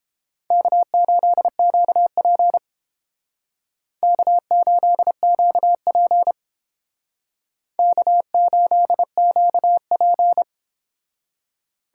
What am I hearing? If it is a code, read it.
Morse code, heard as a keyed tone, about 31 wpm, 700 Hz. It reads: K8QP K8QP K8QP